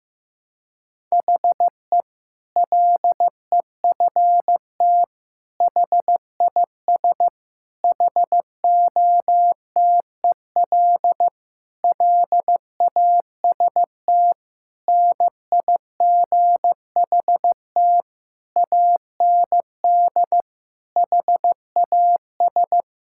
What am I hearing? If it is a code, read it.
Morse code, heard as a keyed tone, about 16 wpm, 700 Hz. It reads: HE LEFT HIS HOTEL LAST NIGHT AND HAS
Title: Sentences from Top 1600 Words 15wpm